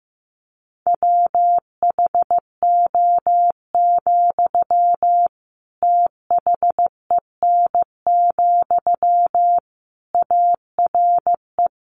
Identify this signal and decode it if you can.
Morse code, heard as a keyed tone, about 15 wpm, 700 Hz. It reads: WHO, THEN, ARE
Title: Sentences from Top 100 Words 15wpm